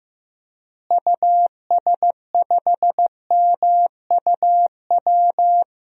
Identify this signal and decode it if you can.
Morse code, heard as a keyed tone, about 15 wpm, 700 Hz. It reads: US5MUW
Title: DXCC-100 Call Signs - 6 Characters - Repeated 3x - 15wpm